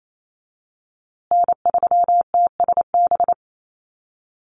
Morse code, heard as a keyed tone, about 28 wpm, 700 Hz. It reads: N3TSB
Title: USA Call Signs Repeated 2x 28wpm